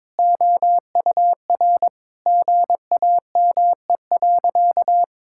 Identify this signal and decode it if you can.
Morse code, heard as a keyed tone, about 22 wpm, 700 Hz. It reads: OUR GAME.
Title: Sentences from Top 1000 Words - Review Part 5 - 22wpm